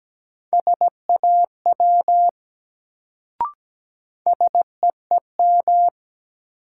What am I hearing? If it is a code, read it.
Morse code, heard as a keyed tone, about 17 wpm, 700 Hz. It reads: SAW SEEM
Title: Sentences from Top 300 Words - New Word Drill - 17wpm